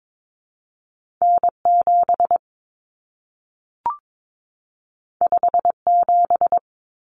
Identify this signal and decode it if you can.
Morse code, heard as a keyed tone, about 22 wpm, 700 Hz. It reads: N7 57
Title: Common Cut Numbers - 2-Digit 22wpm